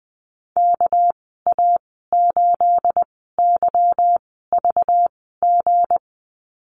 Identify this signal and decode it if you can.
Morse code, heard as a keyed tone, about 20 wpm, spaced as for 14 wpm, 700 Hz. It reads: KA8YVG
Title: 219 Lesson 35g - Number 0 - Call Signs 20wpm